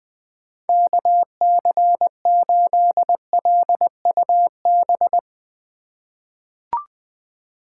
Morse code, heard as a keyed tone, about 20 wpm, 700 Hz. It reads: KC8LUB
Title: 2023 POTA Hunter Call Signs - 20wpm